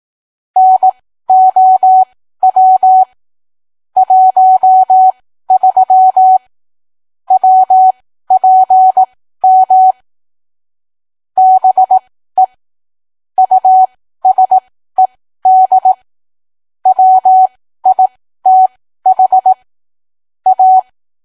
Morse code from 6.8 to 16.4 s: WPM BE USED